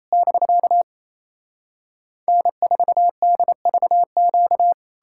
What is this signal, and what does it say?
Morse code, heard as a keyed tone, about 38 wpm, spaced as for 34 wpm, 700 Hz. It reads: <BK> N4DVQ
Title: Speed-Racing Common QSO Phrases - 28wpm